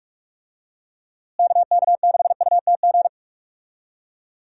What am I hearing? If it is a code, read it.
Morse code, heard as a keyed tone, about 44 wpm, 700 Hz. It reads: KK6UTG
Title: USA Call Signs - 6 Characters - Repeated 3x - 45wpm